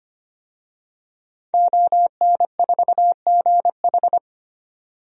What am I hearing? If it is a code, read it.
Morse code, heard as a keyed tone, about 25 wpm, 700 Hz. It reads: ON4GH
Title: DXCC-100 Call Signs - 5 Characters - 25wpm